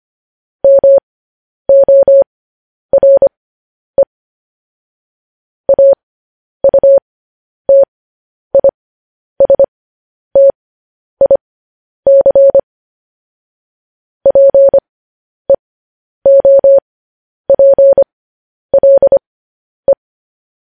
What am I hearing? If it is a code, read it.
Morse code, heard as a keyed tone, about 25 wpm, spaced as for 10 wpm, 550 Hz. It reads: MORE AUTISTIC PEOPLE
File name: podcast / News headlines-1x-25x10